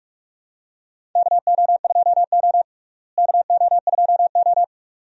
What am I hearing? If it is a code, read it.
Morse code, heard as a keyed tone, about 42 wpm, 700 Hz. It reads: KO2O KO2O
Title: USA Call Signs - 4 Characters - Repeated 2x - 45wpm